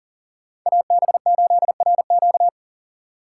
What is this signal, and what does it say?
Morse code, heard as a keyed tone, about 50 wpm, 700 Hz. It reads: AB8RQ